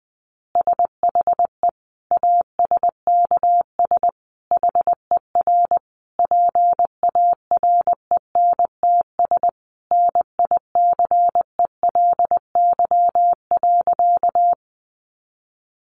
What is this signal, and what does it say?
Morse code, heard as a keyed tone, about 20 wpm, 700 Hz. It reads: SHE ASKS HER PARENTS NICELY.
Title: Easy Phrases 20wpm